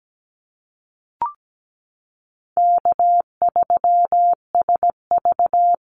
Morse code, heard as a keyed tone, about 17 wpm, 700 Hz. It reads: K3SV
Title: Easy Call Signs - 17wpm